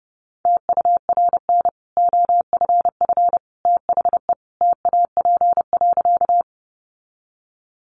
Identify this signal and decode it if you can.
Morse code, heard as a keyed tone, about 30 wpm, 700 Hz. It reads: TURN OFF THE TAP.
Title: Sentences from Top 1400 Words 30wpm